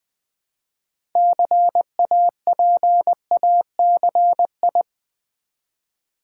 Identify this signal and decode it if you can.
Morse code, heard as a keyed tone, about 21 wpm, 700 Hz. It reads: CAPACI
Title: Build A Word - 2 Letters At A Time - 8-Letter Words - 20wpm